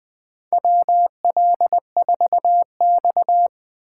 Morse code, heard as a keyed tone, about 20 wpm, 700 Hz. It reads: WL4X
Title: USA Call Signs - 8x Character Spacing - 20wpm